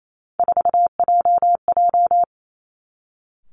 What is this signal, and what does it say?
Morse code, heard as a keyed tone, about 28 wpm, 700 Hz. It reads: 4JJ
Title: Three Challenging Letter-Number Combinations 28wpm